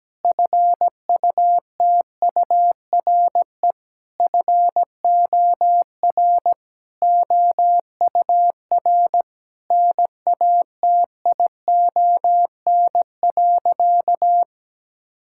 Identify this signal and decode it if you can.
Morse code, heard as a keyed tone, about 17 wpm, 700 Hz. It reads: FUTURE FOR OUR NATION.
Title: Sentences from Top 1000 Words - Review Part 6 - 17wpm